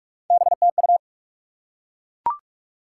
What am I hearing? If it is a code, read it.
Morse code, heard as a keyed tone, about 44 wpm, 700 Hz. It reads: BTU
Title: Top 90 QSO Elements 45wpm